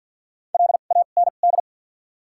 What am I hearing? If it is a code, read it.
Morse code, heard as a keyed tone, about 53 wpm, 700 Hz. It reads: L A N D
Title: 2x Character Spacing - Top 300 Words - 50wpm